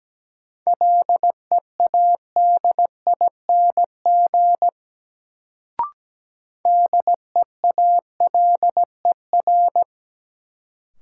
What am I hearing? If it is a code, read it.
Morse code, heard as a keyed tone, about 17 wpm, 700 Hz. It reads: LEADING DEALER
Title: Sentences from Top 1100 Words - New Word Drill - 17wpm